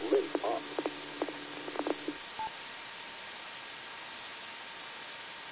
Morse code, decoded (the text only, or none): none